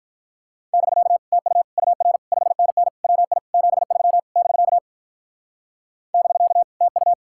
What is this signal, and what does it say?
Morse code, heard as a keyed tone, about 45 wpm, 700 Hz. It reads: <BK> TU UR 5NN WI 73 <BK> <BK> TU